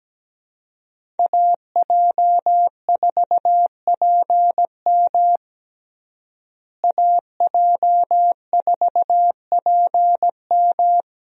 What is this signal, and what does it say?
Morse code, heard as a keyed tone, about 17 wpm, 700 Hz. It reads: AJ4PM AJ4PM
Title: USA Call Signs - 5 Characters - Repeated 3x - 17wpm